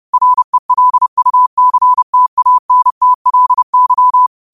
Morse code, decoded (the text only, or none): RELUCTANTLY